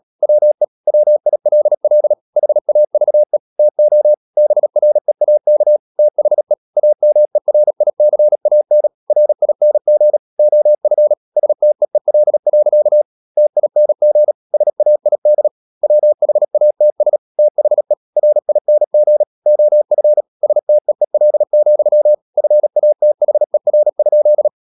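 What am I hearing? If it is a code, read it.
Morse code, heard as a keyed tone, about 37 wpm, 600 Hz. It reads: WE WILL HAVE TO BREAK THE AMERICAN RING OF STEEL. TING SAID WHATS THE RING OF STEEL, FATHER?
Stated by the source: Morse code practice at 35–39 wpm sending English text of Ken Follett's novel Never, from CW med Gnister.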